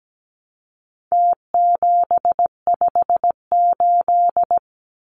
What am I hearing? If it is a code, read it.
Morse code, heard as a keyed tone, about 17 wpm, 700 Hz. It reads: T758
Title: Numbers - Contest Style - 4-Digit 17wpm